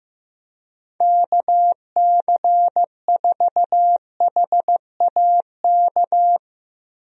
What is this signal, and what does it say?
Morse code, heard as a keyed tone, about 15 wpm, 700 Hz. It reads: KC4HAK
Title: USA Call Signs - 6 Characters - Repeated 2x - 15wpm